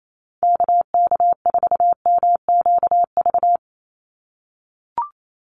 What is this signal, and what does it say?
Morse code, heard as a keyed tone, about 28 wpm, 700 Hz. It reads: KK4MQV
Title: USA Call Signs 28wpm